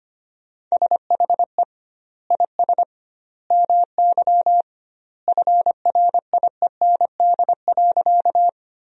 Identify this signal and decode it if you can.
Morse code, heard as a keyed tone, about 25 wpm, 700 Hz. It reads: SHE IS MY FRIEND.
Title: 2x Word-Spacing - Sentences from Top 300 Words - 25wpm